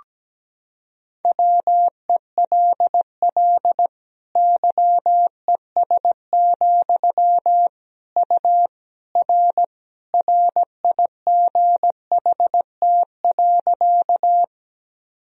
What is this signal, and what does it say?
Morse code, heard as a keyed tone, about 17 wpm, 700 Hz. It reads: WELL YES, U R RIGHT.
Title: Sentences from Top 500 Words and Abbreviations 17wpm